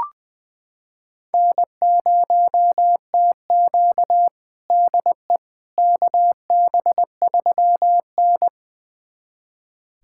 Morse code, heard as a keyed tone, about 20 wpm, 700 Hz. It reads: N0TQ DE KB3N
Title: USA Callsign DE Callsign - 4 Characters - 20wpm